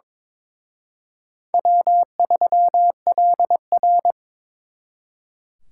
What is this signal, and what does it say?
Morse code, heard as a keyed tone, about 26 wpm, 700 Hz. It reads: W3LR
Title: Speed Racing - USA 1x2 Call Signs 22wpm